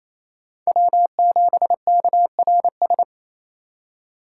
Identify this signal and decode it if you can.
Morse code, heard as a keyed tone, about 28 wpm, 700 Hz. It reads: W7KRS
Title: K1USN SST QSO Practice - 28wpm